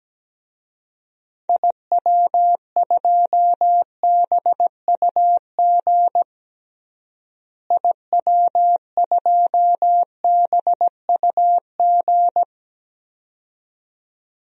Morse code, heard as a keyed tone, about 17 wpm, 700 Hz. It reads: IW2BUG IW2BUG
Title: DXCC-100 Call Signs - 6 Characters - Repeated 2x - 17wpm